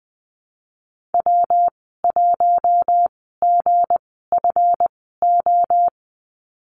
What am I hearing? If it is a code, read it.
Morse code, heard as a keyed tone, about 20 wpm, spaced as for 15 wpm, 700 Hz. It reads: W1GFO
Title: 186 Lesson 30g - Letter b - Call Signs 20wpm